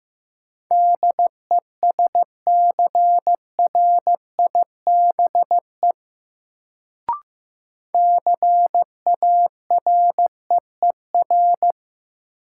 Morse code, heard as a keyed tone, about 15 wpm, 700 Hz. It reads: DESCRIBE CAREER
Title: Sentences from Top 1000 Words - New Word Drill - 15wpm